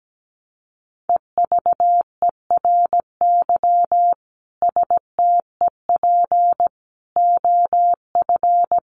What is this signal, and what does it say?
Morse code, heard as a keyed tone, about 17 wpm, 700 Hz. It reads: EVERY STEP OF
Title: Sets of 5 Words Encore2 17wpm